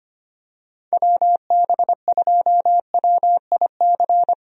Morse code, heard as a keyed tone, about 25 wpm, 700 Hz. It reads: WB2WIC
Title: USA Call Signs - 6 Characters - Repeated 3x - 25wpm